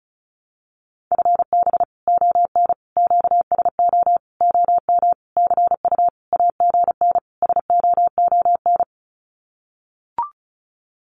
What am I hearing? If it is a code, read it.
Morse code, heard as a keyed tone, about 34 wpm, 700 Hz. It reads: FB ON QSO OM CU AGN SOON
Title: Common QSO Phrases 35wpm